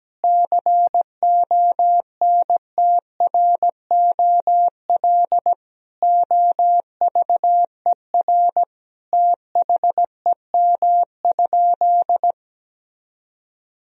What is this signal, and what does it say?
Morse code, heard as a keyed tone, about 17 wpm, 700 Hz. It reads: CONTROL OVER THEM?
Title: Sentences from Top 1000 Words - Review Part 3 - 17wpm